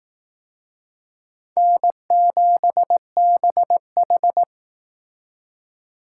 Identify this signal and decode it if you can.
Morse code, heard as a keyed tone, about 18 wpm, 700 Hz. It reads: N7BH
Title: Speed Racing - USA 1x2 Call Signs 15wpm